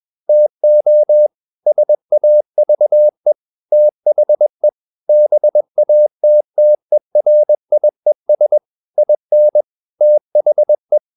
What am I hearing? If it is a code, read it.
Morse code, heard as a keyed tone, about 21 wpm, 600 Hz. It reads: TO SAVE THE BATTERIES IN THE
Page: CW med Gnister